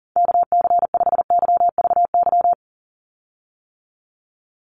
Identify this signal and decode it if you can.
Morse code, heard as a keyed tone, about 40 wpm, 700 Hz. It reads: KC5YVY
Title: USA Call Signs 40wpm